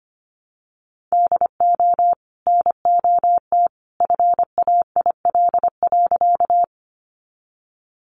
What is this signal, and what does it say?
Morse code, heard as a keyed tone, about 25 wpm, 700 Hz. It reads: DO NOT FAIL.
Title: Sentences from Top 1100 Words 25wpm